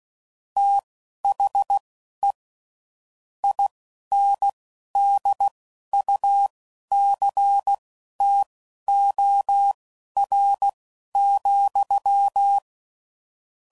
Morse code, heard as a keyed tone, about 16 wpm, spaced as for 11 wpm, 800 Hz. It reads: THE INDUCTOR,
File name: CW Code Groups @10WPM / Technical Subjects / Reactance